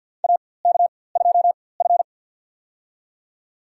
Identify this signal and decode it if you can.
Morse code, heard as a keyed tone, about 51 wpm, spaced as for 23 wpm, 700 Hz. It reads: AK2F
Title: USA Call Signs - 4x Character Spacing - 50wpm